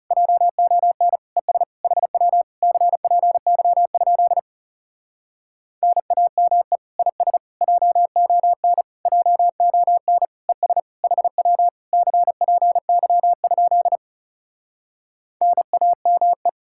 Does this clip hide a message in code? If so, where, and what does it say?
Yes: Morse code from 0.1 to 16.5 s, heard as a keyed tone, about 40 wpm, 700 Hz. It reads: JON ES HW CPY? NAME IS JON JON ES HW CPY? NAME